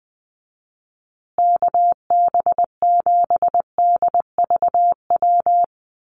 Morse code, heard as a keyed tone, about 20 wpm, 700 Hz. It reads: KB7DVW
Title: USA Call Signs Repeated 3x 20wpm